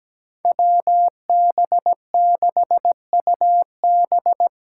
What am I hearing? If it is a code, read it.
Morse code, heard as a keyed tone, about 17 wpm, 700 Hz. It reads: WB6UB
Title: USA Call Signs - 8x Character Spacing - 17wpm